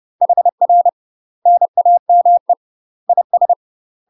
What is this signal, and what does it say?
Morse code, heard as a keyed tone, about 30 wpm, 700 Hz. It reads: HR NAME IS
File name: UK MASTER QSOs TRAINING / 30WPM / UK MASTER QSOs 0034